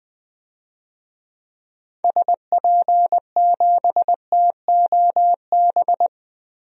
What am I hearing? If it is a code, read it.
Morse code, heard as a keyed tone, about 21 wpm, 700 Hz. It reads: SP7TOB
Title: DXCC-100 Call Signs - 6 Characters - Repeated 3x - 20wpm